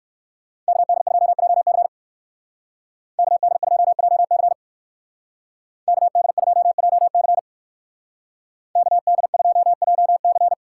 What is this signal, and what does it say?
Morse code, heard as a keyed tone, about 54 wpm, 700 Hz. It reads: KD2JC KD2JC KD2JC KD2JC
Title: Speed Racing - USA 2x2 Call Signs 45wpm